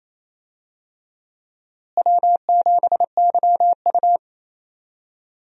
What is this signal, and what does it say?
Morse code, heard as a keyed tone, about 28 wpm, 700 Hz. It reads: W7YU
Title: USA Call Signs - 4 Characters - Repeated 3x - 28wpm